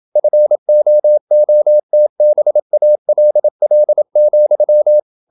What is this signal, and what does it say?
Morse code, heard as a keyed tone, about 27 wpm, 600 Hz. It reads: FOOTBALL,
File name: Never 0027wpm